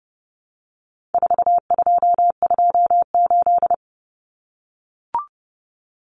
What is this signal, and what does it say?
Morse code, heard as a keyed tone, about 30 wpm, 700 Hz. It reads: 4228
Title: Numbers - 4-Digit 30wpm